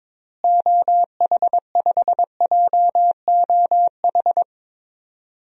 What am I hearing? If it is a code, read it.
Morse code, heard as a keyed tone, about 22 wpm, 700 Hz. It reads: OH5JOH
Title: DXCC-100 Call Signs - 6 Characters - Repeated 2x - 22wpm